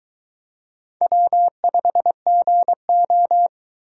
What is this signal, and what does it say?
Morse code, heard as a keyed tone, about 22 wpm, 700 Hz. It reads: W5GO